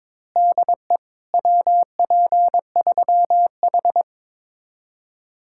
Morse code, heard as a keyed tone, about 22 wpm, 700 Hz. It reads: DE WP3H
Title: Speed-Racing Common QSO Phrases Encore - 22wpm